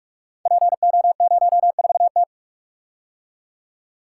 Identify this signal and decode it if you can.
Morse code, heard as a keyed tone, about 44 wpm, 690 Hz. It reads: PO04T
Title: Code Groups - 5 Letter:Numbers - 45wpm